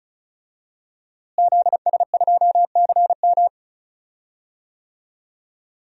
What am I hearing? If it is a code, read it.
Morse code, heard as a keyed tone, about 36 wpm, 700 Hz. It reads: ZS2CM
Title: DXCC-100 Call Signs - 5 Characters - Repeated 2x - 35wpm